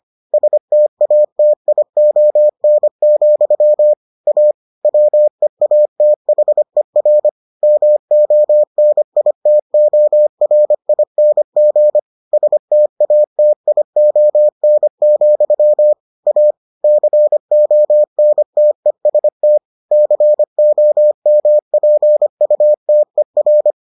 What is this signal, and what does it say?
Morse code, heard as a keyed tone, about 25 wpm, 600 Hz. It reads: STATION, A WEATHER MONITORING STATION, A CONTEST COMPUTER